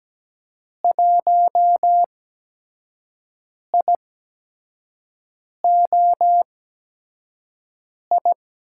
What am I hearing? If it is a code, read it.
Morse code, heard as a keyed tone, about 17 wpm, spaced as for 5 wpm, 700 Hz. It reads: 1IOI
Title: USA Call Signs - 6 Characters - 8x Character Spacing - 17wpm